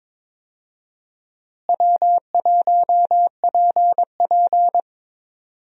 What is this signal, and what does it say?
Morse code, heard as a keyed tone, about 21 wpm, 700 Hz. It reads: W1PP
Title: USA Call Signs - 4 Characters - Repeated 2x - 22wpm